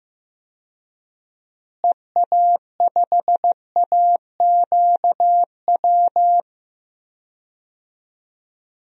Morse code, heard as a keyed tone, about 15 wpm, 700 Hz. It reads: EA5AQW
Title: DXCC-100 Call Signs - 6 Characters - Repeated 3x - 15wpm